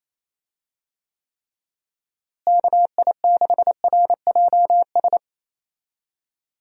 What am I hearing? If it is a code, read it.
Morse code, heard as a keyed tone, about 27 wpm, 700 Hz. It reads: KI6RJS
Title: USA Call Signs - 6 Characters - Repeated 2x - 28wpm